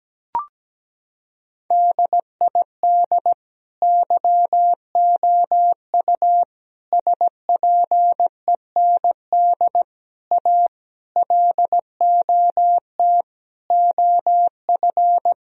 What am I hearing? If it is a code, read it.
Morse code, heard as a keyed tone, about 17 wpm, 700 Hz. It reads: DID YOU SPEND A LOT OF
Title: Sentences from Top 1000 Words - Review Part 4 - 17wpm